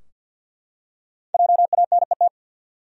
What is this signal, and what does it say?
Morse code, heard as a keyed tone, about 53 wpm, 700 Hz. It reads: JANET
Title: USA Common Names - 50wpm